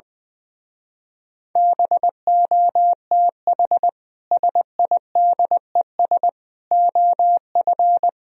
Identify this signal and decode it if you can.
Morse code, heard as a keyed tone, about 21 wpm, 700 Hz. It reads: BOTH SIDES OF
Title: Speed-Racing - Sets of 4 Words 20wpm